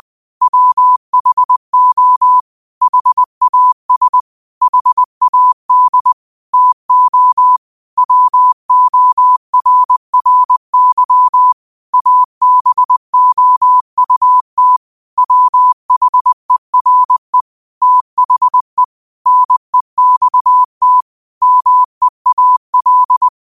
Morse code, heard as a keyed tone, about 20 wpm, 1000 Hz. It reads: WHO HAS HAD TO WORRY ABOUT WHERE THE NEXT MEAL